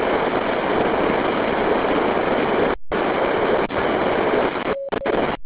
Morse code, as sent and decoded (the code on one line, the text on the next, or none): none